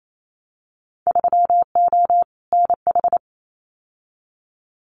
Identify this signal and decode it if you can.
Morse code, heard as a keyed tone, about 28 wpm, 700 Hz. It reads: 3O NH
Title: Winter Field Day Exchanges Encore 28wpm